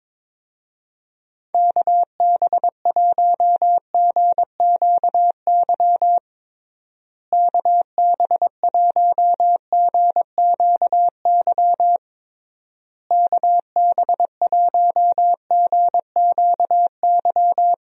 Morse code, heard as a keyed tone, about 22 wpm, 700 Hz. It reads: KB1GQY KB1GQY KB1GQY
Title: USA Call Signs - 6 Characters - Repeated 3x - 22wpm